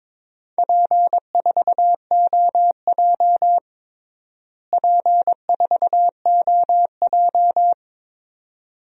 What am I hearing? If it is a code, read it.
Morse code, heard as a keyed tone, about 22 wpm, 700 Hz. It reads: P4OJ P4OJ
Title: DXCC-100 Call Signs - 4 Characters - Repeated 2x - 22wpm